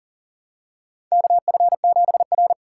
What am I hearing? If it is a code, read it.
Morse code, heard as a keyed tone, about 40 wpm, 700 Hz. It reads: KF7R
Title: USA Callsign DE Callsign 40wpm